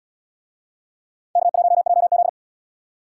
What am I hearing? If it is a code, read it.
Morse code, heard as a keyed tone, about 77 wpm, 700 Hz. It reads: K1JZ